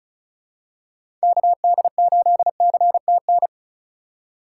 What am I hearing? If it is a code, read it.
Morse code, heard as a keyed tone, about 34 wpm, 700 Hz. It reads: KD8CTN
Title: USA Call Signs - 6 Characters - 2x Character Spacing - 35wpm